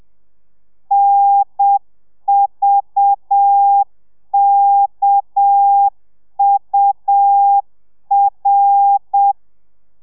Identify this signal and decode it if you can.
Morse code, heard as a keyed tone, about 7 wpm, 800 Hz. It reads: NVKUR